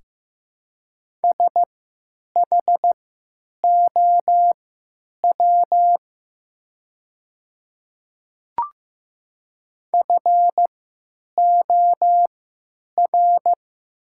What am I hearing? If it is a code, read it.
Morse code, heard as a keyed tone, about 15 wpm, spaced as for 9 wpm, 700 Hz. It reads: SHOW FOR
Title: 3x Character Spacing - Top 200 Words - 15wpm